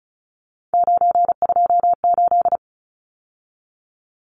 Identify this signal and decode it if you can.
Morse code, heard as a keyed tone, about 34 wpm, 700 Hz. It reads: 928
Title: Numbers - 3-Digit 35wpm